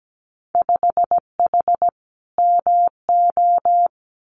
Morse code, heard as a keyed tone, about 17 wpm, 700 Hz. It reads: 5H MO
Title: Winter Field Day Exchanges Encore 17wpm